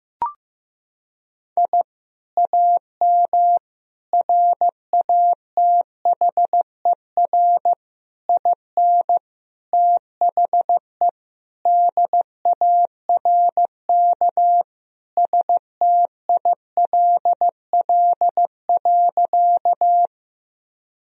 Morse code, heard as a keyed tone, about 15 wpm, 700 Hz. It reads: I AM RATHER IN THE DARK STILL.
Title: Sentences from Top 1000 Words - Review Part 4 - 15wpm